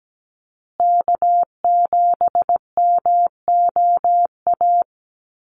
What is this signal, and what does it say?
Morse code, heard as a keyed tone, about 17 wpm, 700 Hz. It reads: K7MOA
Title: Call Signs Repeated 3x - 17wpm